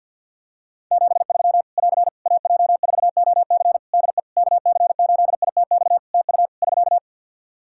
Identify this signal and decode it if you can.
Morse code, heard as a keyed tone, about 51 wpm, 700 Hz. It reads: 73 <AR> AJ4OC DE KC8ITX TU <SK>
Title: Practice QSO - Exchange 4 - Ending - 50wpm